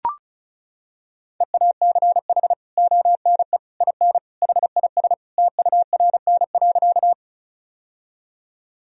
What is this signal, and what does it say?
Morse code, heard as a keyed tone, about 36 wpm, 700 Hz. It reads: EACH ONE IN HIS TURN.
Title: Sentences from Top 1000 Words - Review Part 1 - 35wpm